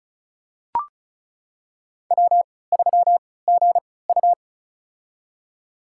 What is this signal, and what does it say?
Morse code, heard as a keyed tone, about 36 wpm, spaced as for 20 wpm, 700 Hz. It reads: W3GU
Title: USA Call Signs - 3x Character Spacing - 35wpm